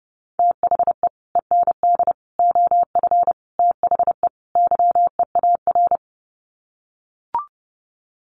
Morse code, heard as a keyed tone, about 30 wpm, 700 Hz. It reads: THE END OF THE YEAR
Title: Sets of 5 Words 30wpm